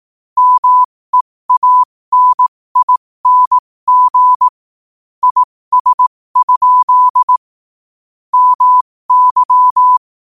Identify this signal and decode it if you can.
Morse code, heard as a keyed tone, about 18 wpm, spaced as for 15 wpm, 1000 Hz. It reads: MEANING IS? MY
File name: qotdfiles / 15wpm